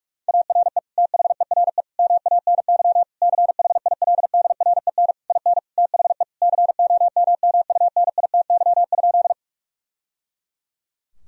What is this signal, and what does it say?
Morse code, heard as a keyed tone, about 41 wpm, 700 Hz. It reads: ARE THERE MANY CHILDREN IN THE COMMUNITY?
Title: Sentences from Top 1000 Words - Review Part 4 - 45wpm